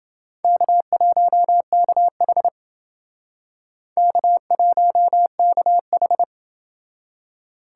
Morse code, heard as a keyed tone, about 29 wpm, 700 Hz. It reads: K1KH K1KH